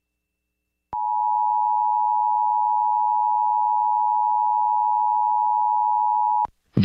Emergency Alert System (SAME) tones (the 853 and 960 Hz attention tone). It is heard from 0.9 to 6.5 s.